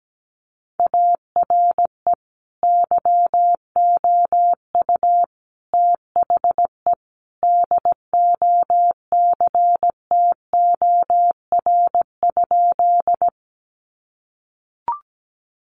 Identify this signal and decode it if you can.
Morse code, heard as a keyed tone, about 17 wpm, 700 Hz. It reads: ARE YOU THE DOCTOR?
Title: Sentences from Top 1500 Words 17wpm